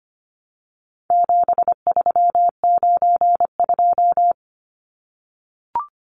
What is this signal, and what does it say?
Morse code, heard as a keyed tone, about 25 wpm, 700 Hz. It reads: 7392
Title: Numbers - 4-Digit 25wpm